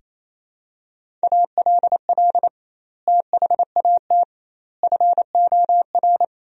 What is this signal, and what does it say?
Morse code, heard as a keyed tone, about 28 wpm, 700 Hz. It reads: ALL THAT FOR
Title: 2x Word-Spacing - Sentences from Top 500 Words - 28wpm